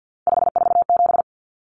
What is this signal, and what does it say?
Morse code, heard as a keyed tone, about 50 wpm, 700 Hz. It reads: 547
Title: Common Cut Numbers - 3-Digit 50wpm